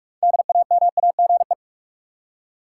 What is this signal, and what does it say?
Morse code, heard as a keyed tone, about 42 wpm, 700 Hz. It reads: DAMAGE
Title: Six Letter Words - 45wpm